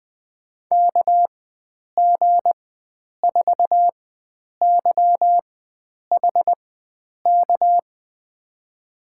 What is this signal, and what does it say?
Morse code, heard as a keyed tone, about 20 wpm, spaced as for 9 wpm, 700 Hz. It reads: KG4YHK
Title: USA Call Signs - 6 Characters - 4x Character Spacing - 20wpm